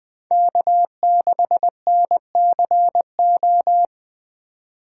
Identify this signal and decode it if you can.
Morse code, heard as a keyed tone, about 20 wpm, 700 Hz. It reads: K6NCO
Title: USA Call Signs - 5 Characters - 4x Character Spacing - 20wpm